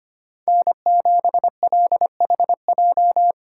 Morse code, heard as a keyed tone, about 25 wpm, 700 Hz. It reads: N7LHJ
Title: USA Call Signs - 4x Character Spacing - 25wpm